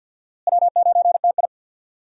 Morse code, heard as a keyed tone, about 51 wpm, 700 Hz. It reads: W9TI
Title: USA Call Signs - 4 Characters - Repeated 3x - 50wpm